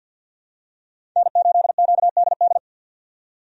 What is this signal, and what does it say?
Morse code, heard as a keyed tone, about 45 wpm, 700 Hz. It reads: N8QDD
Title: USA Call Signs - 5 Characters - 50wpm